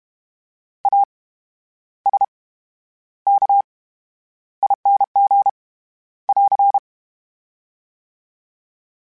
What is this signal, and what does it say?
Morse code, heard as a keyed tone, about 32 wpm, spaced as for 11 wpm, 800 Hz. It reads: ASKING <AR>
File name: Special INGs / Special INGs-mp3s / VerySlowToFaster Ing @30